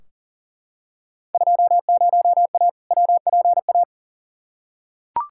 Morse code, heard as a keyed tone, about 40 wpm, 700 Hz. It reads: 20A WPA
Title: Field Day Exchanges Encore - 40wpm